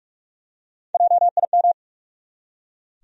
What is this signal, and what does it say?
Morse code, heard as a keyed tone, about 42 wpm, 700 Hz. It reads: JIM
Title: USA Common Names - Max 3 Letters - 45wpm